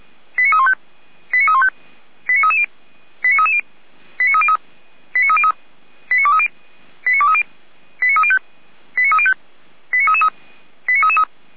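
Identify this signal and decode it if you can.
Five-tone selective call (ZVEI), heard as quick runs of five tones, tones of about 70 ms. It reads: 89315 89315 89330 89330 89303 89303 89230 89230 89305 89305 89333 89333
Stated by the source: Five-tone ZVEI selective call received by SDR.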